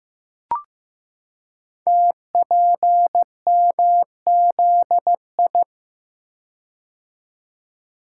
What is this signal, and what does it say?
Morse code, heard as a keyed tone, about 15 wpm, 700 Hz. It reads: TPMZI
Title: Code Groups - 5 Letters - 15wpm